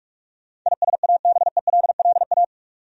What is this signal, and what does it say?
Morse code, heard as a keyed tone, about 46 wpm, 700 Hz. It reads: ISABELLA
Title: USA Common Names - 45wpm